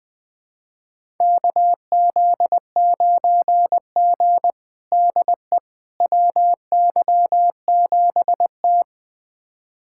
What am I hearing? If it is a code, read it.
Morse code, heard as a keyed tone, about 20 wpm, 700 Hz. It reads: KZ9G DE WY7T
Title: USA Callsign DE Callsign - 4 Characters - 20wpm